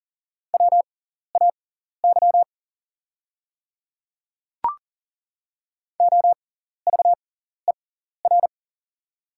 Morse code, heard as a keyed tone, about 40 wpm, spaced as for 14 wpm, 690 Hz. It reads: WAY OVER
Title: 6x Character Spacing - Top 100 Words - 40wpm